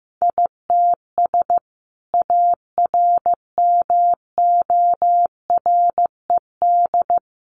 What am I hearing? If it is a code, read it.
Morse code, heard as a keyed tone, about 15 wpm, 700 Hz. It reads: ITS ARMORED
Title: A Sound Of Thunder 15wpm